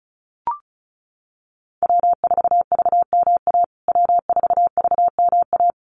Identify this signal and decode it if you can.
Morse code, heard as a keyed tone, about 34 wpm, 700 Hz. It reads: W4VMA W4VMA
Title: USA Call Signs Repeated 2x 35wpm